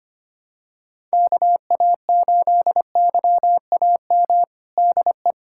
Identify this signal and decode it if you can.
Morse code, heard as a keyed tone, about 25 wpm, 700 Hz. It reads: KA8YAM DE
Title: USA Callsign DE Callsign - 6 Characters - 25wpm